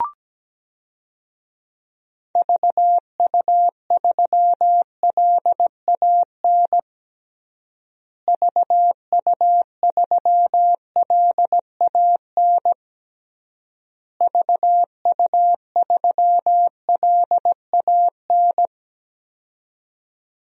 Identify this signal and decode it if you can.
Morse code, heard as a keyed tone, about 17 wpm, 700 Hz. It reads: VU3LAN VU3LAN VU3LAN
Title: DXCC-100 Call Signs - 6 Characters - Repeated 3x - 17wpm